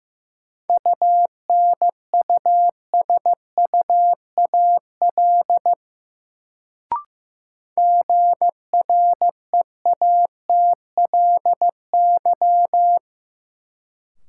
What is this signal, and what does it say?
Morse code, heard as a keyed tone, about 15 wpm, 700 Hz. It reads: UNUSUAL GREATLY
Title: Sentences from Top 1400 Words - New Word Drill Alt - 15wpm